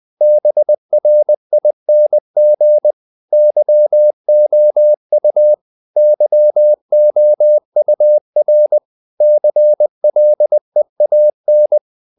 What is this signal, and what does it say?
Morse code, heard as a keyed tone, about 20 wpm, 600 Hz. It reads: BRING YOU YOUR CLEAN
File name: the old man and the sea part 2 20wpm